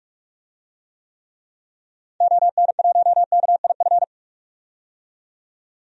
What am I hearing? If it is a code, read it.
Morse code, heard as a keyed tone, about 44 wpm, 690 Hz. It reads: ON1KIF
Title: DXCC-100 Call Signs - 6 Characters - 45wpm